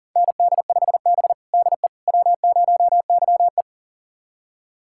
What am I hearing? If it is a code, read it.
Morse code, heard as a keyed tone, about 37 wpm, 700 Hz. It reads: ND5B DE W0YE
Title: USA Callsign DE Callsign - 4 Characters - 40wpm